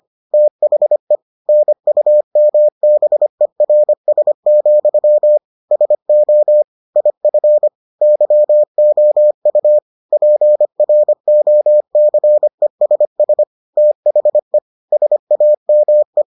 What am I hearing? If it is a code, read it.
Morse code, heard as a keyed tone, about 25 wpm, 600 Hz. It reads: THE NUMBERS, SO IF YOU PROCESS THE SAME